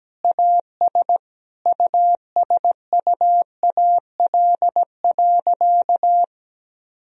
Morse code, heard as a keyed tone, about 17 wpm, 700 Hz. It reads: AS USUAL.
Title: Sentences from Top 1000 Words - Review Part 2 - 17wpm